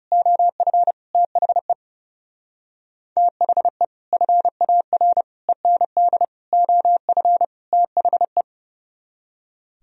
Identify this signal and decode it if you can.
Morse code, heard as a keyed tone, about 37 wpm, 700 Hz. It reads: OF THE THE FAR END OF THE
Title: Speed-Racing - Sets of 5 Words 30wpm